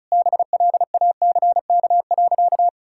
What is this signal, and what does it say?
Morse code, heard as a keyed tone, about 36 wpm, 700 Hz. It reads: BLACK.
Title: Sentences from Top 1600 Words 35wpm